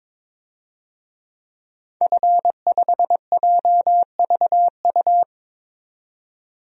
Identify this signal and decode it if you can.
Morse code, heard as a keyed tone, about 22 wpm, 700 Hz. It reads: F5JVU
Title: DXCC-100 Call Signs - 5 Characters - 22wpm